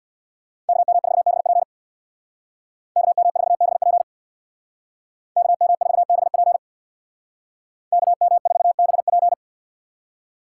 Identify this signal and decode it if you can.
Morse code, heard as a keyed tone, about 64 wpm, 690 Hz. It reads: KM4BP KM4BP KM4BP KM4BP